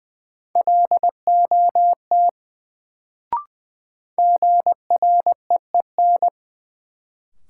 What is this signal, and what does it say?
Morse code, heard as a keyed tone, about 21 wpm, 700 Hz. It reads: LOT GREEN
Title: Sentences from Top 500 Words - New Word Drill - 20wpm